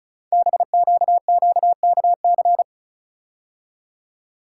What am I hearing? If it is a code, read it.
Morse code, heard as a keyed tone, about 36 wpm, 690 Hz. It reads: BQQKC
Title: Code Groups - 5 Letters - 35wpm